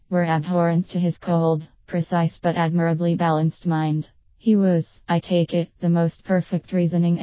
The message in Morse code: none